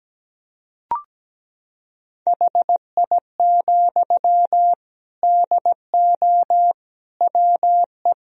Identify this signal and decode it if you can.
Morse code, heard as a keyed tone, about 17 wpm, 700 Hz. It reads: HI, DO WE
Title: Sentences from Top 1000 Words - Review Part 3 - 17wpm